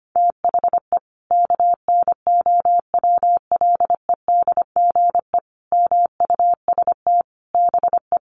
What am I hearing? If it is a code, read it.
Morse code, heard as a keyed tone, about 25 wpm, 700 Hz. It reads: THE KNOWLEDGE MUST BE
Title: Sentences from Top 1400 Words 25wpm